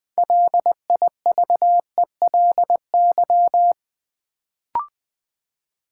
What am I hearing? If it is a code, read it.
Morse code, heard as a keyed tone, about 20 wpm, 700 Hz. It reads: LIVELY
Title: Build A Word - 2 Letters At A Time - 6-Letter Words - 20wpm